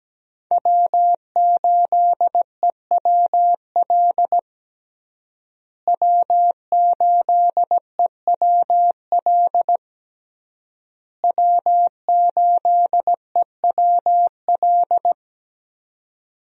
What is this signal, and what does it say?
Morse code, heard as a keyed tone, about 17 wpm, 700 Hz. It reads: W8EWL W8EWL W8EWL
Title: USA Call Signs - 5 Characters - Repeated 3x - 17wpm